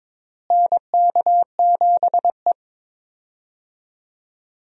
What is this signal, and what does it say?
Morse code, heard as a keyed tone, about 21 wpm, 700 Hz. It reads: NK7E
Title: USA Call Signs - 4 Characters - Repeated 2x - 22wpm